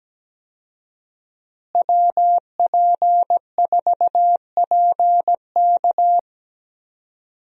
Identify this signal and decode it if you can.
Morse code, heard as a keyed tone, about 17 wpm, 700 Hz. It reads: WP4PK
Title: USA Call Signs - 5 Characters - Repeated 2x - 17wpm